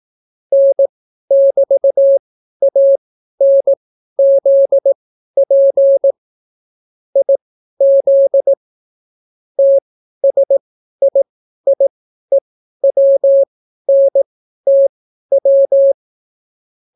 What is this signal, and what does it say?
Morse code, heard as a keyed tone, about 18 wpm, spaced as for 12 wpm, 550 Hz. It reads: N<BT>ANZP IZ TSIIEWNTW